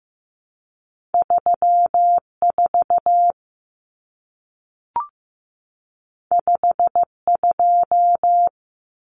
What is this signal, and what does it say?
Morse code, heard as a keyed tone, about 15 wpm, 700 Hz. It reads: 34 52
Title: Numbers - 2-Digit 15wpm